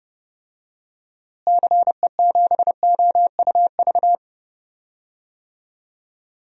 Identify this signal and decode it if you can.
Morse code, heard as a keyed tone, about 31 wpm, 700 Hz. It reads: CE7OUV
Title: DXCC-100 Call Signs - 6 Characters - Repeated 3x - 30wpm